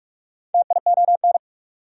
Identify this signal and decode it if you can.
Morse code, heard as a keyed tone, about 44 wpm, 690 Hz. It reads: TION
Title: Prefixes and Suffixes 45wpm